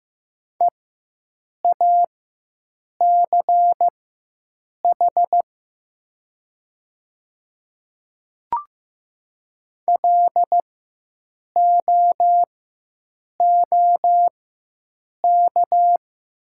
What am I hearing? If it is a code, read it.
Morse code, heard as a keyed tone, about 15 wpm, spaced as for 7 wpm, 700 Hz. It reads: EACH LOOK
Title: 4x Character Spacing - Top 100 Words - 15wpm